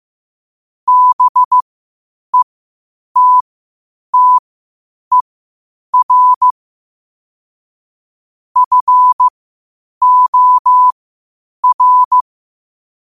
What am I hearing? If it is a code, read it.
Morse code, heard as a keyed tone, about 15 wpm, spaced as for 8 wpm, 1000 Hz. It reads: BETTER FOR
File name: qotdfiles / 8wpm